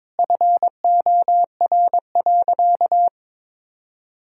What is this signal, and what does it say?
Morse code, heard as a keyed tone, about 22 wpm, 700 Hz. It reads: FOR.
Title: Sentences from Top 1000 Words - Review Part 3 - 22wpm